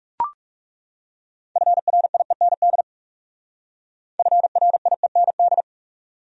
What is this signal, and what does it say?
Morse code, heard as a keyed tone, about 38 wpm, 700 Hz. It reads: FRIEND FRIEND
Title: Speed-Racing Top 300 Words 30wpm